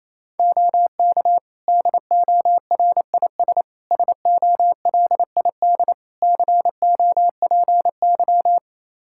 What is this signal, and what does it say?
Morse code, heard as a keyed tone, about 28 wpm, 700 Hz. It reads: OK DORIS SOLID COPY
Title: Practice QSO - Exchange 3 - More Info - 28wpm